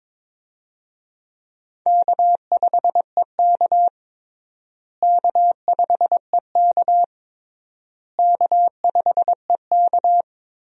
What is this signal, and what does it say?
Morse code, heard as a keyed tone, about 22 wpm, 700 Hz. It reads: K5EK K5EK K5EK
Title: USA Call Signs - 4 Characters - Repeated 3x - 22wpm